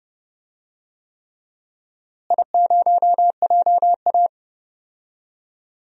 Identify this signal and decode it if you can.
Morse code, heard as a keyed tone, about 32 wpm, 700 Hz. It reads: I0JA
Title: DXCC-100 Call Signs - 4 Characters - Repeated 3x - 30wpm